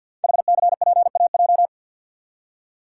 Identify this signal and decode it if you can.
Morse code, heard as a keyed tone, about 52 wpm, 700 Hz. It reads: HCPAJ
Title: Code Groups - 5 Letters - 50wpm